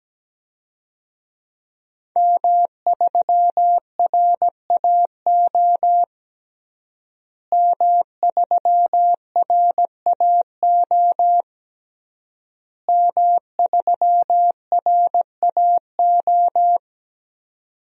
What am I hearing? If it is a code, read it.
Morse code, heard as a keyed tone, about 17 wpm, 700 Hz. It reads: M3RAO M3RAO M3RAO
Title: DXCC-100 Call Signs - 5 Characters - Repeated 3x - 17wpm